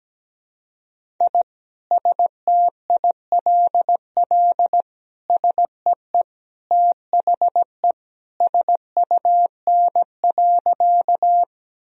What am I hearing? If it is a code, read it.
Morse code, heard as a keyed tone, about 17 wpm, 700 Hz. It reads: I STILL SEE THE SUN.
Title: Sentences from Top 1000 Words - Review Part 1 - 17wpm